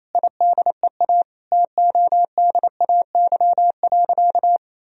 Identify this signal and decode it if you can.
Morse code, heard as a keyed tone, about 28 wpm, 700 Hz. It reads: IDEA TODAY.
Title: Sentences from Top 1000 Words - Review Part 4 - 28wpm